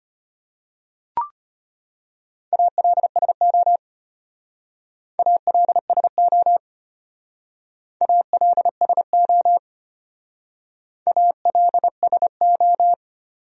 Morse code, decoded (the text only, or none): ALSO ALSO ALSO ALSO